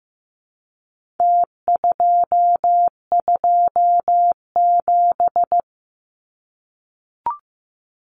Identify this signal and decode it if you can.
Morse code, heard as a keyed tone, about 15 wpm, 700 Hz. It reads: T227
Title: Numbers - Contest Style - 4-Digit 15wpm